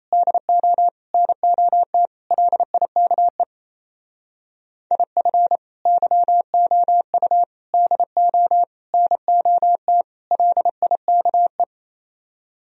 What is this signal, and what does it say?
Morse code, heard as a keyed tone, about 35 wpm, 700 Hz. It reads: DO NOT LIKE IF YOU DO NOT LIKE
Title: Speed-Racing - Sets of 5 Words 28wpm